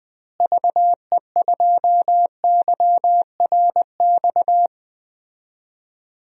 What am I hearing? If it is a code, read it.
Morse code, heard as a keyed tone, about 20 wpm, 700 Hz. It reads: VE2YRX
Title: DXCC-100 Call Signs - 6 Characters - Repeated 3x - 20wpm